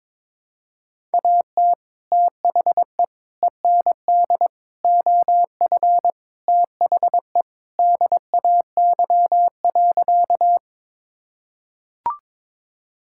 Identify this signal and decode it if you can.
Morse code, heard as a keyed tone, about 21 wpm, 700 Hz. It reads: AT THE END OF THE DAY.
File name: practice / 22wpm / Sentences with 3-Word Phrases - 22wpm